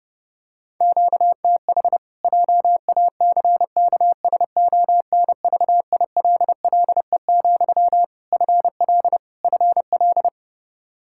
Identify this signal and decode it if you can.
Morse code, heard as a keyed tone, about 32 wpm, 700 Hz. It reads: QTH JACKSONVILLE, FL FL
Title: Practice QSO - Exchange 1 - Beginning - 30wpm